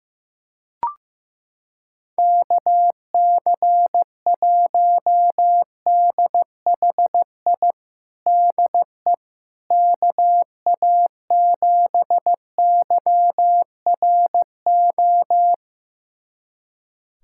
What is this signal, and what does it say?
Morse code, heard as a keyed tone, about 15 wpm, 700 Hz. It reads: KC1DHI DE KA7YRO
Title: USA Callsign DE Callsign - 6 Characters - 15wpm